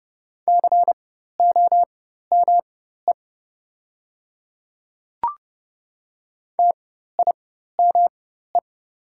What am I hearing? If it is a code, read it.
Morse code, heard as a keyed tone, about 30 wpm, spaced as for 14 wpm, 700 Hz. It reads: COME TIME
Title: 4x Character Spacing - Top 100 Words - 30wpm